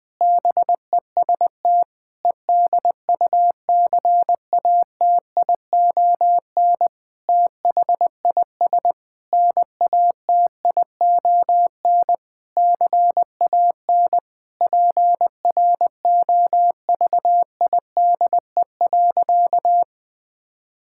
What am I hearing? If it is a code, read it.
Morse code, heard as a keyed tone, about 21 wpm, 700 Hz. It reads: BEST EDUCATION THIS NATION CAN PROVIDE.
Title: Sentences from Top 1000 Words - Review Part 6 - 20wpm